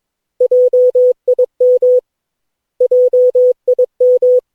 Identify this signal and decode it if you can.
Morse code, heard as a keyed tone, about 22 wpm, 500 Hz. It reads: JIM JIM
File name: cw / Session9 cw training